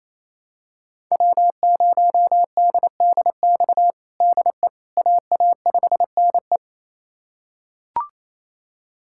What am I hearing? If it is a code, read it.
Morse code, heard as a keyed tone, about 28 wpm, 700 Hz. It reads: W0DDX DE AA5NE
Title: USA Callsign DE Callsign - 5 Characters - 28wpm